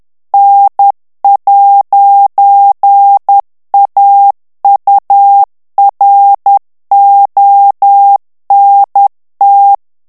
Morse code, heard as a keyed tone, about 10 wpm, 800 Hz. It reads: N'AURONT
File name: FRANCAIS TEXTE CLAIR 2 VITESSE 10 MOTS MINUTES